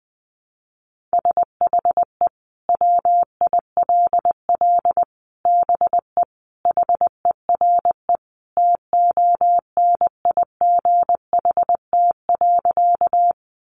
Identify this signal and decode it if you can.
Morse code, heard as a keyed tone, about 20 wpm, 700 Hz. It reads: SHE WILL BE HERE TONIGHT.
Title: Sentences from Top 600 Words 20wpm